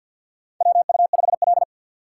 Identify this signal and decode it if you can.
Morse code, heard as a keyed tone, about 51 wpm, 700 Hz. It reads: WU5L
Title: USA Call Signs - 4 Characters - 2x Character Spacing - 50wpm